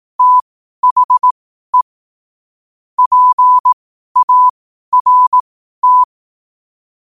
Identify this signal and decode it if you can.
Morse code, heard as a keyed tone, about 18 wpm, spaced as for 12 wpm, 1000 Hz. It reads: THE PART
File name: qotdfiles / 12wpm